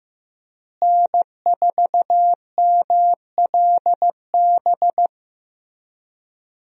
Morse code, heard as a keyed tone, about 15 wpm, 700 Hz. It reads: N4MLB
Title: USA Call Signs - 5 Characters - Repeated 3x - 15wpm